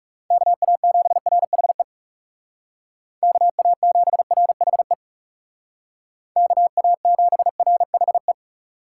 Morse code, heard as a keyed tone, about 39 wpm, 700 Hz. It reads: KA7RHE KA7RHE KA7RHE